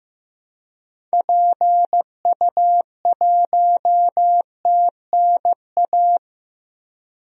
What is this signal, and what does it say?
Morse code, heard as a keyed tone, about 16 wpm, 700 Hz. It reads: PU1TNA
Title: DXCC-100 Call Signs - 6 Characters - Repeated 2x - 15wpm